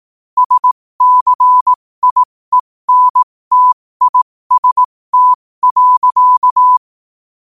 Morse code, heard as a keyed tone, about 18 wpm, spaced as for 15 wpm, 1000 Hz. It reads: SCIENTIST.
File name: qotdfiles / 15wpm